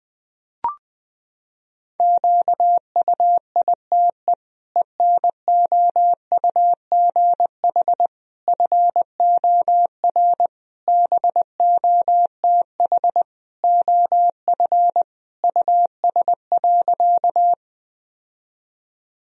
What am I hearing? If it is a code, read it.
Morse code, heard as a keyed tone, about 21 wpm, 700 Hz. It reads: QUITE ENOUGH FOR BOTH OF US.
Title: Sentences from Top 1000 Words - Review Part 1 - 20wpm